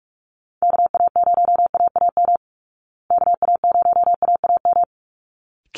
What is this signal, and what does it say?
Morse code, heard as a keyed tone, about 44 wpm, 700 Hz. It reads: KA0AAM KA0AAM
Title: Call Signs Repeated 3x - 45wpm